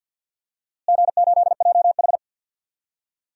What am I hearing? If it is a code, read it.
Morse code, heard as a keyed tone, about 52 wpm, 700 Hz. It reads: G8JH
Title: DXCC-100 Call Signs - 4 Characters - Repeated 3x - 50wpm